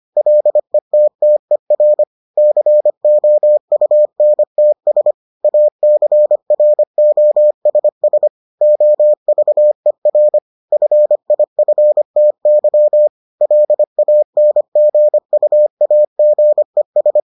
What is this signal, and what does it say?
Morse code, heard as a keyed tone, about 25 wpm, 600 Hz. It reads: LETTER COUNTS ACROSS OVER FIFTY LANGUAGES